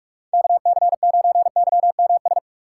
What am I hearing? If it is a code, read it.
Morse code, heard as a keyed tone, about 48 wpm, 700 Hz. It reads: KC9YMS
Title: USA Call Signs - 2x Character Spacing - 45wpm